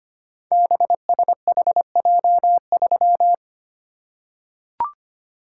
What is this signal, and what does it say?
Morse code, heard as a keyed tone, about 25 wpm, 700 Hz. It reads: BSHJ3
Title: Code Groups - 5 Letter:Numbers - 25wpm